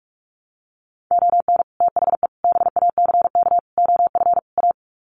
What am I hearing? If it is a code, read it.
Morse code, heard as a keyed tone, about 44 wpm, 700 Hz. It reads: ON THE BACK OF A
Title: Sets of 5 Words Encore2 45wpm